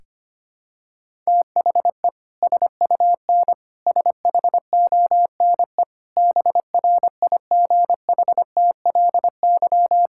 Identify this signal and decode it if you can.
Morse code, heard as a keyed tone, about 25 wpm, 700 Hz. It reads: THE SUN SHONE BRIGHTLY
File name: practice / 25wpm / Sentences with 2-Word Phrases - 25wpm